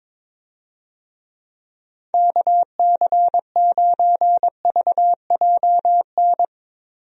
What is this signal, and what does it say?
Morse code, heard as a keyed tone, about 22 wpm, 700 Hz. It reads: KC9VJN
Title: USA Call Signs - 6 Characters - Repeated 2x - 22wpm